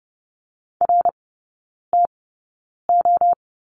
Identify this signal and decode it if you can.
Morse code, heard as a keyed tone, about 30 wpm, spaced as for 9 wpm, 700 Hz. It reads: RTO
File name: Lesson 18e - Letter m - Three Characters 30wpm